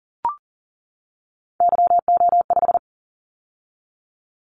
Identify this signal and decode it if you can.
Morse code, heard as a keyed tone, about 40 wpm, 700 Hz. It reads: YO5
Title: Callsign Prefixes for Top 25 DXCC - 40wpm